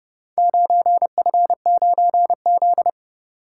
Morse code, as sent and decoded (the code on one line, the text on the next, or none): ----. ..-. ----. --..
9F9Z